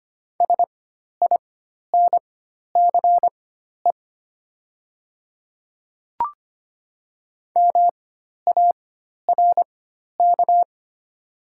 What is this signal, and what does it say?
Morse code, heard as a keyed tone, about 25 wpm, spaced as for 12 wpm, 700 Hz. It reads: SINCE MARK
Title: 4x Character Spacing - Top 300 Words - 25wpm